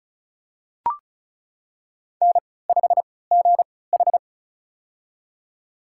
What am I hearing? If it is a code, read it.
Morse code, heard as a keyed tone, about 36 wpm, spaced as for 20 wpm, 700 Hz. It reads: N5GH
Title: USA Call Signs - 4 Characters - 3x Character Spacing - 35wpm